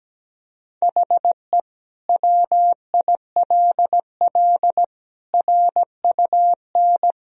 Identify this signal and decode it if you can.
Morse code, heard as a keyed tone, about 17 wpm, 700 Hz. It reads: HE WILL RUN
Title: Sentences from Top 1000 Words - Review Part 2 - 17wpm